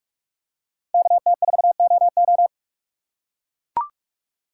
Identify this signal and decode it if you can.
Morse code, heard as a keyed tone, about 41 wpm, 700 Hz. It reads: KT4OO
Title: USA Call Signs - 2x Character Spacing - 45wpm